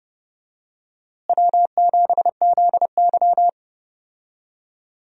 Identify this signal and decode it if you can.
Morse code, heard as a keyed tone, about 30 wpm, 700 Hz. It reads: W7ZY
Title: USA Call Signs - 4 Characters - Repeated 3x - 30wpm